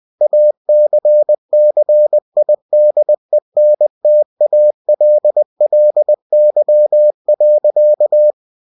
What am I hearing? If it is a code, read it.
Morse code, heard as a keyed tone, about 20 wpm, 600 Hz. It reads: ACCIDENTALLY.
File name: cw-books / winnie-the-pooh / winnie-20wpm-page-0107